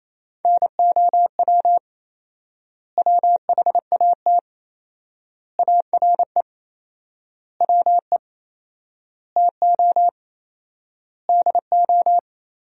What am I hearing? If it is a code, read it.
Morse code, heard as a keyed tone, about 27 wpm, 700 Hz. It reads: NOW WHAT ARE WE TO DO
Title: 4x Word-Spacing - Sentences from Top 100 Words - 28wpm